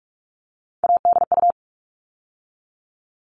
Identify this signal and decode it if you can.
Morse code, heard as a keyed tone, about 44 wpm, 700 Hz. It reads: ADU
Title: All Cut Numbers - 3-Digit 45wpm